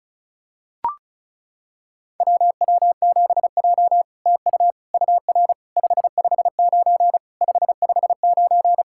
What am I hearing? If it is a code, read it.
Morse code, heard as a keyed tone, about 36 wpm, 700 Hz. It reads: WW7J TU UR 559 559
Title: SOTA QSO Practice 35wpm